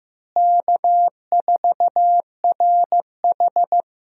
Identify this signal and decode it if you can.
Morse code, heard as a keyed tone, about 15 wpm, 700 Hz. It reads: K4RH
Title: USA Call Signs - 4 Characters - Repeated 2x - 15wpm